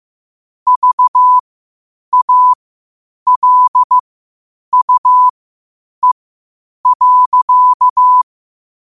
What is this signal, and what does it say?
Morse code, heard as a keyed tone, about 15 wpm, spaced as for 8 wpm, 1000 Hz. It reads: VALUE.